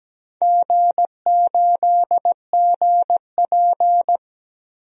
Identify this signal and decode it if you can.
Morse code, heard as a keyed tone, about 17 wpm, 700 Hz. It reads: G8GP
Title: DXCC-100 Call Signs - 4 Characters - Repeated 2x - 17wpm